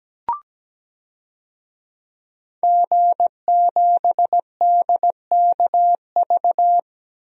Morse code, heard as a keyed tone, about 17 wpm, 700 Hz. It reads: G7DKV
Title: DXCC-100 Call Signs - 5 Characters - 17wpm